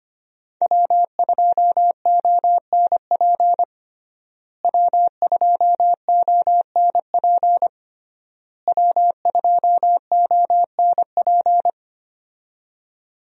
Morse code, heard as a keyed tone, about 25 wpm, 700 Hz. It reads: W2ONP W2ONP W2ONP
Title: USA Call Signs - 5 Characters - Repeated 3x - 25wpm